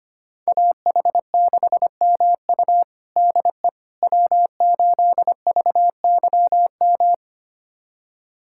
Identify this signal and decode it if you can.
Morse code, heard as a keyed tone, about 25 wpm, 700 Hz. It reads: AH6MU DE W8VYM
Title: USA Callsign DE Callsign - 5 Characters - 25wpm